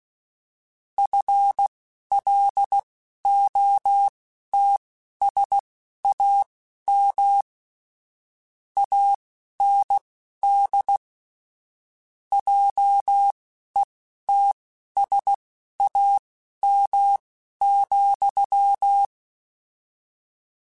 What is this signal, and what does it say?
Morse code, heard as a keyed tone, about 16 wpm, spaced as for 11 wpm, 800 Hz. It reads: FLOTSAM AND JETSAM,
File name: Misc / Misc-mp3s / Word Pairs @10